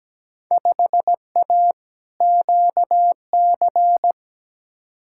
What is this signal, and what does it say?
Morse code, heard as a keyed tone, about 17 wpm, 700 Hz. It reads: 5A QC
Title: Field Day Exchanges Encore - 17wpm